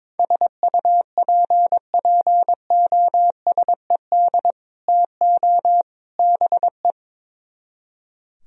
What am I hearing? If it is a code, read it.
Morse code, heard as a keyed tone, about 22 wpm, 700 Hz. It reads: SUPPOSED TO BE
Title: Speed-Racing - Sets of 3 Words 22wpm